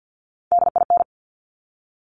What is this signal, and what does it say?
Morse code, heard as a keyed tone, about 50 wpm, 700 Hz. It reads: DIN
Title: Top Three Letter Combinations in Words 50wpm